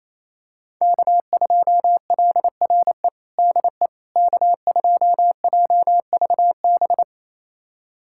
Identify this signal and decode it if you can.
Morse code, heard as a keyed tone, about 28 wpm, 700 Hz. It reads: K2LRE DE K2JVB
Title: USA Callsign DE Callsign - 5 Characters - 28wpm